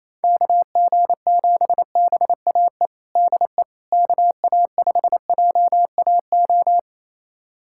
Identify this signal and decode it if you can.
Morse code, heard as a keyed tone, about 28 wpm, 700 Hz. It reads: KG7BAE DE KA5JAO
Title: USA Callsign DE Callsign - 6 Characters - 28wpm